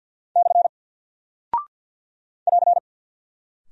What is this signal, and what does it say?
Morse code, heard as a keyed tone, about 48 wpm, 700 Hz. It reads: / <AR>
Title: Single Character (Letter:Number:Punctuation:Prosign) Alt 50wpm